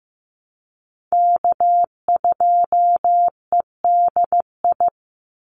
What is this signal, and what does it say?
Morse code, heard as a keyed tone, about 15 wpm, 700 Hz. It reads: K2EDI
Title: USA Call Signs 15wpm